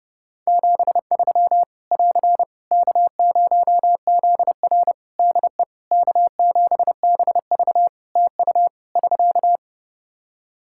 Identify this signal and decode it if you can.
Morse code, heard as a keyed tone, about 30 wpm, 700 Hz. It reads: 73 <AR> K0ZR DE K7BV TU <SK>
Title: K1USN SST QSO Practice 30wpm